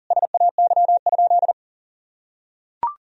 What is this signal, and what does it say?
Morse code, heard as a keyed tone, about 40 wpm, 700 Hz. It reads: SAY?
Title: Sentences from Top 1000 Words - Review Part 1 - 40wpm